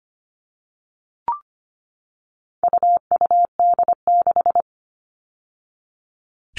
Morse code, heard as a keyed tone, about 25 wpm, 700 Hz. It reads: UUD6
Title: All Cut Numbers - 4-Digit 25wpm